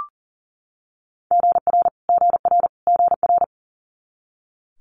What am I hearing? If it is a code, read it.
Morse code, heard as a keyed tone, about 40 wpm, 700 Hz. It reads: GR GR GR
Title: Multi-letter Phonemes Repeated 3x 40wpm